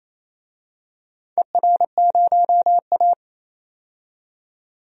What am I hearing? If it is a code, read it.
Morse code, heard as a keyed tone, about 27 wpm, 690 Hz. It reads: ER0A
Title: DXCC-100 Call Signs - 4 Characters - 28wpm